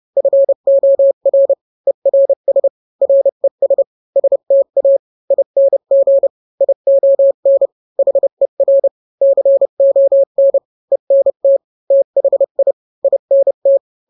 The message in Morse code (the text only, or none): FOR ERS RES STA ING ION HER CON ENT THI INT